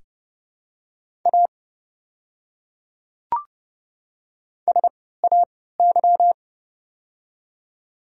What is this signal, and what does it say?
Morse code, heard as a keyed tone, about 30 wpm, spaced as for 17 wpm, 700 Hz. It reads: A SAY
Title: 3x Character Spacing - Top 300 Words - 30wpm